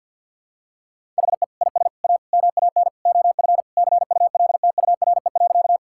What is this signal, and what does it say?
Morse code, heard as a keyed tone, about 47 wpm, 700 Hz. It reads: HE IS A MAN OF CULTURE.
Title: Sentences from Top 1000 Words - Review Part 6 - 50wpm